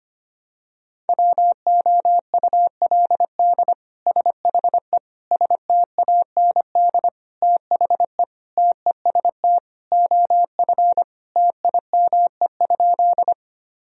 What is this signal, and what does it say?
Morse code, heard as a keyed tone, about 25 wpm, 700 Hz. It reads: WOULD SHE STAND THE TEST OF TIME?
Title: Sentences from Top 1000 Words - Review Part 3 - 25wpm